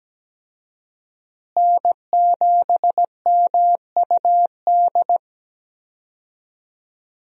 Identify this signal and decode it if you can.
Morse code, heard as a keyed tone, about 17 wpm, 700 Hz. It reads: N7MUD
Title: USA Call Signs - 5 Characters - Repeated 2x - 17wpm